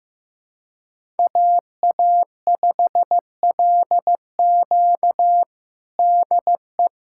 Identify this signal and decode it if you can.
Morse code, heard as a keyed tone, about 15 wpm, 700 Hz. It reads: AA5LQ DE
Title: USA Callsign DE Callsign - 5 Characters - 15wpm